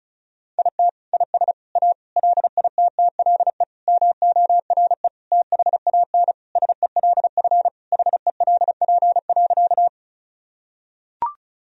Morse code, heard as a keyed tone, about 34 wpm, 700 Hz. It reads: IT IS A LITTLE MORE THAN SELF HELP.
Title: Sentences from Top 1000 Words - Review Part 1 - 35wpm